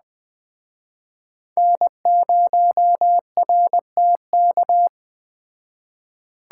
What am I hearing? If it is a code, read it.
Morse code, heard as a keyed tone, about 21 wpm, 700 Hz. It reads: N0RTK